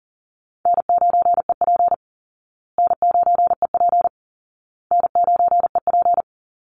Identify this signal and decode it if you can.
Morse code, heard as a keyed tone, about 40 wpm, 700 Hz. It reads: N9EP N9EP N9EP
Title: Easy Call Signs Repeated 3x - 40wpm